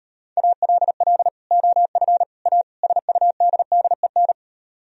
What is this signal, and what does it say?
Morse code, heard as a keyed tone, about 38 wpm, 700 Hz. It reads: ALL OF A SUDDEN